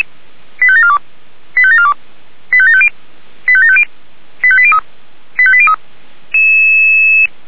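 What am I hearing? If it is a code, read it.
Five-tone selective call (ZVEI), heard as quick runs of five tones, tones of about 70 ms. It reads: 85742 85742 85750 85750 85803 85803
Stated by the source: ZVEI five-tone call received by SDR.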